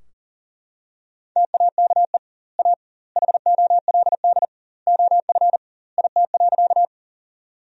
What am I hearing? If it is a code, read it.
Morse code, heard as a keyed tone, about 40 wpm, 700 Hz. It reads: TAKE A HOLD OF IT.
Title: 2x Word-Spacing - Sentences from Top 400 Words - 40wpm